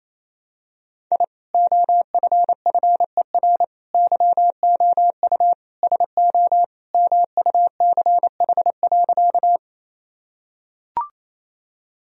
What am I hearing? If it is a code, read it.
Morse code, heard as a keyed tone, about 28 wpm, 700 Hz. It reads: I OFFER YOU SO MUCH.
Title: Sentences from Top 1000 Words - Review Part 4 - 28wpm